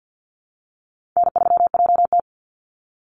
Morse code, heard as a keyed tone, about 48 wpm, 700 Hz. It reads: N3JT
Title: Call Signs Repeated 2x - 50wpm